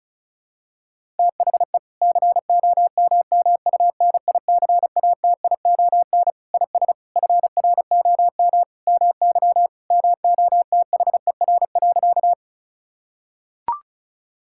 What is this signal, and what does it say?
Morse code, heard as a keyed tone, about 36 wpm, 700 Hz. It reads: THE COMMUNICATION IS FROM MY MOTHER.
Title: Sentences from Top 1000 Words - Review Part 6 - 35wpm